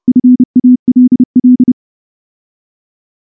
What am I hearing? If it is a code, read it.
Morse code, heard as a keyed tone, about 28 wpm, 250 Hz. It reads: FALL
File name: morse / morse-training-30WPM